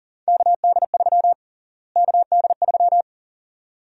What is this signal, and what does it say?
Morse code, heard as a keyed tone, about 40 wpm, 700 Hz. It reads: KD3 KD3
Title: USA Call Sign Prefixes - Repeated 2x - 40wpm